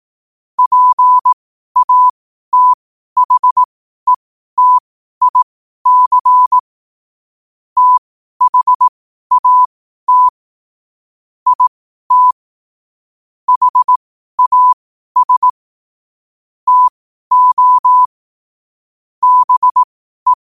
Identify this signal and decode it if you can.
Morse code, heard as a keyed tone, about 18 wpm, spaced as for 12 wpm, 1000 Hz. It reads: PATHETIC THAT IT HAS TO BE
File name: qotdfiles / 12wpm